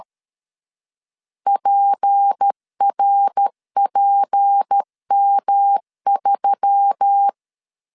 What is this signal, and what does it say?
Morse code, heard as a keyed tone, about 13 wpm, 800 Hz. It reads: PRPM3
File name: LEZIONI / 6 PKX0 / 6 P 12wpm 2033